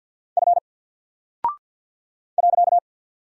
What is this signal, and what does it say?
Morse code, heard as a keyed tone, about 50 wpm, 700 Hz. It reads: F .
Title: Single Character (Letter:Number:Punctuation:Prosign) 50wpm